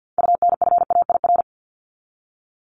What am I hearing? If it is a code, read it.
Morse code, heard as a keyed tone, about 48 wpm, 700 Hz. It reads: UNFAIR
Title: Top 1751-2000 Words 50wpm